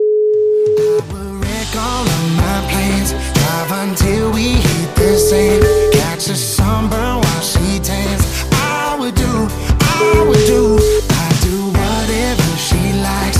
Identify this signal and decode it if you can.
Ringback tone (Europe, 425 Hz), with noise.